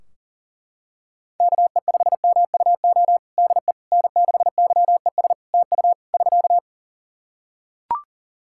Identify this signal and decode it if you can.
Morse code, heard as a keyed tone, about 37 wpm, 700 Hz. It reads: KE5MUO DE N6YES TU <SK>
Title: K1USN SST QSO Practice 40wpm